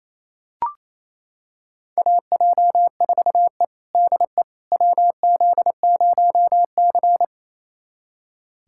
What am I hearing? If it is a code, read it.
Morse code, heard as a keyed tone, about 28 wpm, 700 Hz. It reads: AJ4E DE WZ0C
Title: USA Callsign DE Callsign - 4 Characters - 28wpm